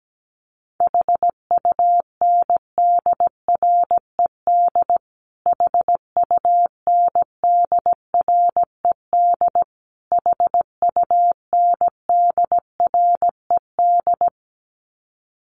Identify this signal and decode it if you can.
Morse code, heard as a keyed tone, about 17 wpm, 700 Hz. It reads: HUNDRED HUNDRED HUNDRED
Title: Top 500 Words Repeated 17wpm